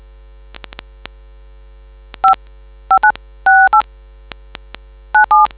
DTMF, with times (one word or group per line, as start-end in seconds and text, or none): 2.2-2.3 5
2.9-3.8 5968
5.1-5.5 97